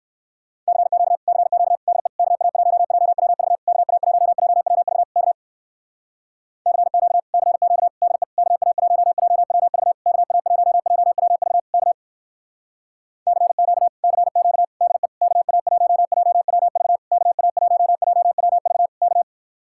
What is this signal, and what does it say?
Morse code, heard as a keyed tone, about 64 wpm, 700 Hz. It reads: CQ CQ DE KA1JWV KA1JWV K CQ CQ DE KA1JWV KA1JWV K CQ CQ DE KA1JWV KA1JWV K